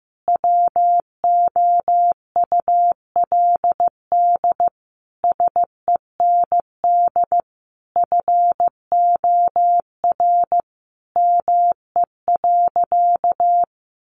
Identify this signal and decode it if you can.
Morse code, heard as a keyed tone, about 15 wpm, 700 Hz. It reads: WOULD SEND FOR ME.
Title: Sentences from Top 1500 Words 15wpm